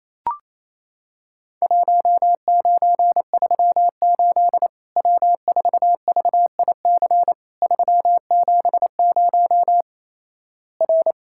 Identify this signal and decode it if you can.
Morse code, heard as a keyed tone, about 28 wpm, 700 Hz. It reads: 1938 W4VIC 370 R
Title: Worked All Europe DX Contest - QTC Exchange - 28wpm